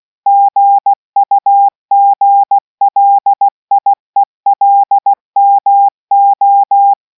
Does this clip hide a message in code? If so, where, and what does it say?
Yes: Morse code from 0.3 to 6.9 s, heard as a keyed tone, about 16 wpm, 800 Hz. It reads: GUGLIELMO